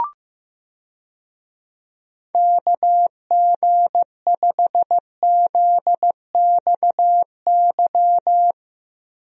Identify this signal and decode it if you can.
Morse code, heard as a keyed tone, about 15 wpm, 700 Hz. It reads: KG5ZXY
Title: USA Call Signs - 6 Characters - Repeated 2x - 15wpm